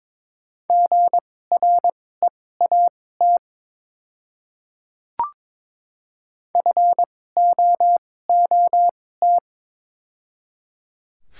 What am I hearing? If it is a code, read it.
Morse code, heard as a keyed tone, about 23 wpm, spaced as for 16 wpm, 700 Hz. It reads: GREAT FOOT
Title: 2x Character Spacing - Top 500 Words - 22wpm